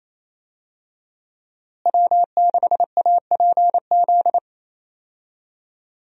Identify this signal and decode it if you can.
Morse code, heard as a keyed tone, about 28 wpm, 700 Hz. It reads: W6APZ
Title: USA Call Signs - 5 Characters - Repeated 3x - 28wpm